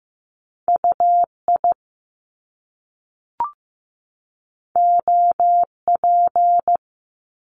Morse code, heard as a keyed tone, about 15 wpm, 700 Hz. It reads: UI OP
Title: Top Two Letter Combinations in Words 15wpm